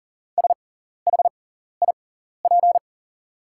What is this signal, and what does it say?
Morse code, heard as a keyed tone, about 40 wpm, spaced as for 14 wpm, 690 Hz. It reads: SHIP
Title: 6x Character Spacing - Top 500 Words - 40wpm